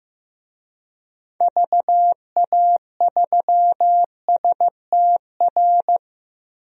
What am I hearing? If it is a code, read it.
Morse code, heard as a keyed tone, about 15 wpm, 700 Hz. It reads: VA3STR
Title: DXCC-100 Call Signs - 6 Characters - 15wpm